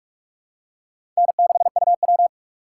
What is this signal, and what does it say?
Morse code, heard as a keyed tone, about 44 wpm, 700 Hz. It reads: N6UW
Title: USA Call Signs - 4 Characters - 3x Character Spacing - 45wpm